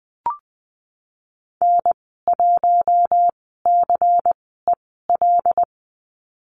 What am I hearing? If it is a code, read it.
Morse code, heard as a keyed tone, about 20 wpm, spaced as for 14 wpm, 700 Hz. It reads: N1CEL
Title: 179 Lesson 29g - Slash - Call Signs 20wpm